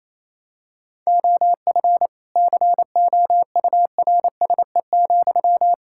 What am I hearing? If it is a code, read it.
Morse code, heard as a keyed tone, about 28 wpm, 700 Hz. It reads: OF COURSE,
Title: Sentences from Top 1600 Words 28wpm